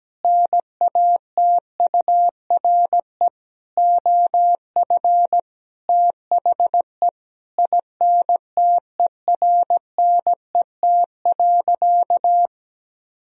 Morse code, heard as a keyed tone, about 17 wpm, 700 Hz. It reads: NATURE OF THE INTERNET.
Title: Sentences from Top 1000 Words - Review Part 4 - 17wpm